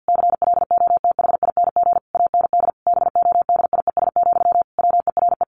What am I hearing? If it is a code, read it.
Morse code, heard as a keyed tone, about 50 wpm, 700 Hz. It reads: CLOTHING AND BODIES, WERE
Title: A Sound Of Thunder 50wpm